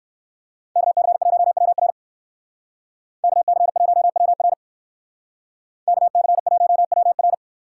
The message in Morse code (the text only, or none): KC1WR KC1WR KC1WR